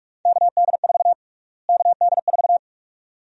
Morse code, heard as a keyed tone, about 46 wpm, 700 Hz. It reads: KD4 KD4
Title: USA Call Sign Prefixes - Repeated 2x - 45wpm